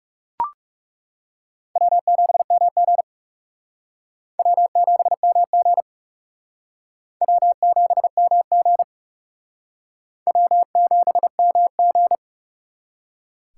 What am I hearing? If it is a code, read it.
Morse code, heard as a keyed tone, about 38 wpm, 700 Hz. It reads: W7MG W7MG W7MG W7MG